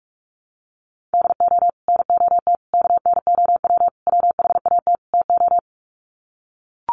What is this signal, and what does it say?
Morse code, heard as a keyed tone, about 44 wpm, 700 Hz. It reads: DO NOT KNOW WHAT TO
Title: Sets of 5 Words 45wpm